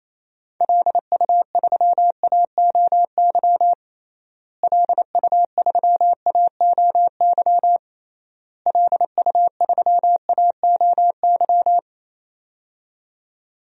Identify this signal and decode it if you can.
Morse code, heard as a keyed tone, about 28 wpm, 700 Hz. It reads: LU3AOY LU3AOY LU3AOY
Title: DXCC-100 Call Signs - 6 Characters - Repeated 3x - 28wpm